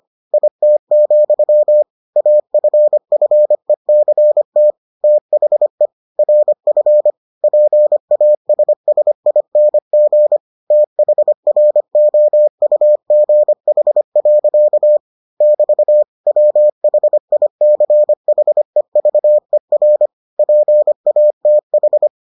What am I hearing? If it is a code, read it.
Morse code, heard as a keyed tone, about 25 wpm, 600 Hz. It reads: IT, AFFECT THE RF PASSING THROUGH. <BT> WHICHEVER PATH